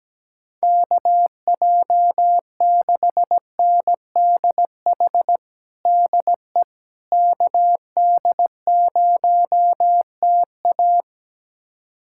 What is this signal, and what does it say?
Morse code, heard as a keyed tone, about 17 wpm, 700 Hz. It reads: KJ6NDH DE KD0TA
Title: USA Callsign DE Callsign 17wpm